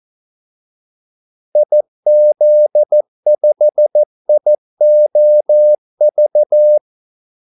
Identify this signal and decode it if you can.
Morse code, heard as a keyed tone, about 14 wpm, 600 Hz. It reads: IZ5IOV
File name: Callsigns-14wpm